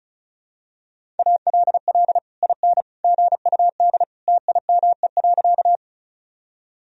Morse code, heard as a keyed tone, about 36 wpm, 700 Hz. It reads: ALL IN GUD TIME.
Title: Sentences from Top 200 Words and Abbreviations 35wpm